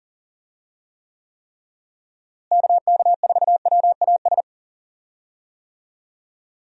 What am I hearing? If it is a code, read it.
Morse code, heard as a keyed tone, about 40 wpm, 700 Hz. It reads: KK4WAS
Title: USA Call Signs - 6 Characters - Repeated 3x - 40wpm